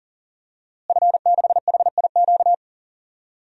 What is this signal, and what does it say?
Morse code, heard as a keyed tone, about 40 wpm, 700 Hz. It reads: F6HIQ
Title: DXCC-100 Call Signs - 5 Characters - Repeated 2x - 40wpm